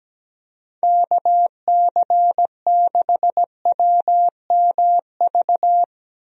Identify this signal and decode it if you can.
Morse code, heard as a keyed tone, about 17 wpm, 700 Hz. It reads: KC6WMV
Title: USA Call Signs - 6 Characters - Repeated 3x - 17wpm